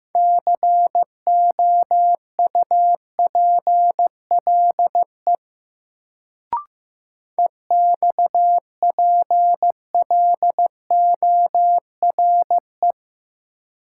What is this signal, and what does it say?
Morse code, heard as a keyed tone, about 15 wpm, 700 Hz. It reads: COUPLE EXPLORE
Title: Sentences from Top 1100 Words - New Word Drill Alt - 15wpm